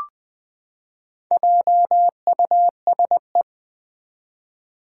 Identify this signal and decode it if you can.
Morse code, heard as a keyed tone, about 20 wpm, 700 Hz. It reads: JUSE
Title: Four Challenging Letter-Number Combinations 20wpm